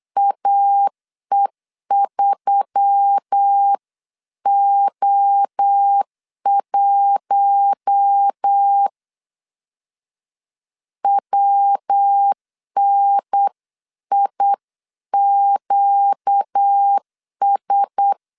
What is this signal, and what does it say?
Morse code, heard as a keyed tone, about 8 wpm, 800 Hz. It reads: AE3O1 WNIQS
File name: CW / DET / 1 6 1500C S B 8wpm 5142